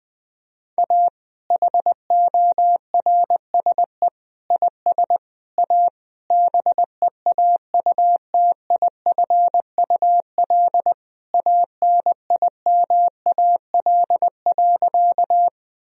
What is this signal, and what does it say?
Morse code, heard as a keyed tone, about 20 wpm, 700 Hz. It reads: A HORSE IS A BEAUTIFUL ANIMAL.
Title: Sentences from Top 1000 Words - Review Part 1 - 20wpm